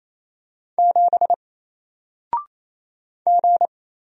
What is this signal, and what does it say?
Morse code, heard as a keyed tone, about 27 wpm, 700 Hz. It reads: 7 G
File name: Single-Letter-Number-28wpm